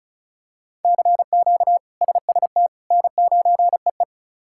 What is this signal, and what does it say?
Morse code, heard as a keyed tone, about 36 wpm, 700 Hz. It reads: CQ SST N9EE
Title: K1USN SST QSO Practice - 35wpm